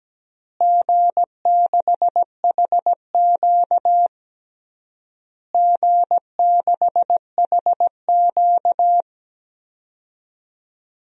Morse code, heard as a keyed tone, about 17 wpm, 700 Hz. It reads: G6HQ G6HQ
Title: DXCC-100 Call Signs - 4 Characters - Repeated 3x - 17wpm